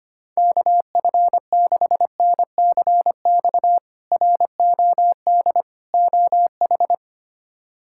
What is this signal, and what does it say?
Morse code, heard as a keyed tone, about 25 wpm, 700 Hz. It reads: KF6NCX ROD OH
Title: K1USN SST QSO Practice - 25wpm